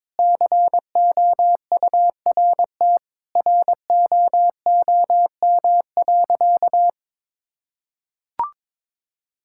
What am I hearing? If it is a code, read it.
Morse code, heard as a keyed tone, about 22 wpm, 700 Hz. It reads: COURTROOM.
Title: Sentences from Top 1600 Words 22wpm